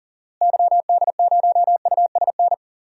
Morse code, heard as a keyed tone, about 39 wpm, 700 Hz. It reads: YD0USN
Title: DXCC-100 Call Signs - 6 Characters - Repeated 3x - 40wpm